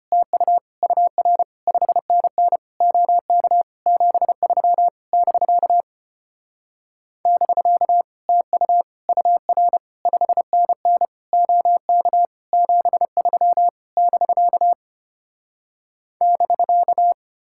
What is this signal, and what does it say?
Morse code, heard as a keyed tone, about 33 wpm, 700 Hz. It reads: TU UR 5NN OK 73 <BK> <BK> TU UR 5NN OK 73 <BK> <BK>